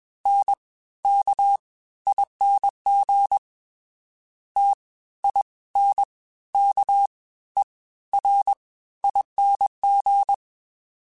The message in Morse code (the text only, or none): NKING TINKERING